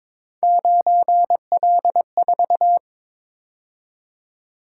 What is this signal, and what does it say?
Morse code, heard as a keyed tone, about 22 wpm, 700 Hz. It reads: 9L4
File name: practice / 22wpm / Three Letter-Number Combinations - 22wpm